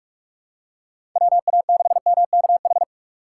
Morse code, heard as a keyed tone, about 44 wpm, 700 Hz. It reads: WA6MKH
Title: USA Call Signs - 6 Characters - 6x Character Spacing - 45wpm